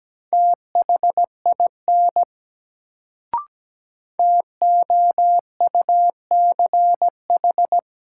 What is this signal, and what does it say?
Morse code, heard as a keyed tone, about 17 wpm, 700 Hz. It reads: THIN TOUCH
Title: Sentences from Top 1200 Words - New Word Drill - 17wpm